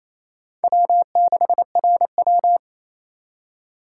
Morse code, heard as a keyed tone, about 27 wpm, 700 Hz. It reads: W6RW
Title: USA Call Signs - 4 Characters - 2x Character Spacing - 28wpm